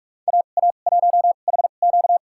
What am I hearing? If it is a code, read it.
Morse code, heard as a keyed tone, about 43 wpm, spaced as for 32 wpm, 700 Hz. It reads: AA1HQ
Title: USA Call Signs - 2x Character Spacing - 45wpm